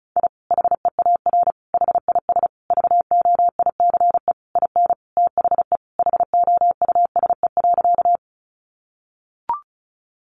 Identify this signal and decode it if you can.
Morse code, heard as a keyed tone, about 34 wpm, 700 Hz. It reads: I HEAR HIS VOICE IN THE HOUSE.
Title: Sentences from Top 500 Words 35wpm